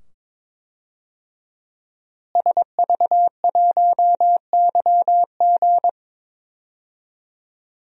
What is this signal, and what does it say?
Morse code, heard as a keyed tone, about 22 wpm, 700 Hz. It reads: SV1YG
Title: DXCC-100 Call Signs - 5 Characters - Repeated 2x - 22wpm